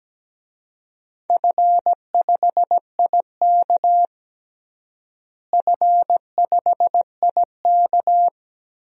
Morse code, heard as a keyed tone, about 17 wpm, 700 Hz. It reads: F5IK F5IK
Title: DXCC-100 Call Signs - 4 Characters - Repeated 2x - 17wpm